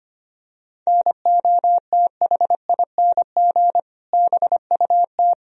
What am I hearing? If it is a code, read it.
Morse code, heard as a keyed tone, about 25 wpm, 700 Hz. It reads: NOTHING BUT
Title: Sentences from Top 1000 Words - Review Part 1 - 25wpm